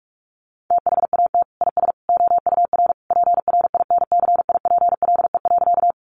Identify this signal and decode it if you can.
Morse code, heard as a keyed tone, about 44 wpm, 700 Hz. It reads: THAT IS OUR PRINCIPLE.
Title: Sentences from Top 1400 Words 45wpm